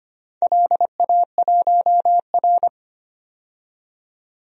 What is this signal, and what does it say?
Morse code, heard as a keyed tone, about 25 wpm, 700 Hz. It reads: LA1R
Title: DXCC-100 Call Signs - 4 Characters - 25wpm